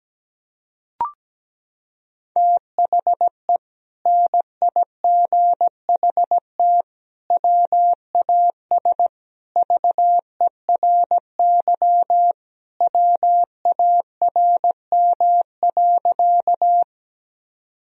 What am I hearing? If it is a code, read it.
Morse code, heard as a keyed tone, about 17 wpm, 700 Hz. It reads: THE NIGHT WAS VERY WARM.
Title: Sentences from Top 1000 Words - Review Part 2 - 17wpm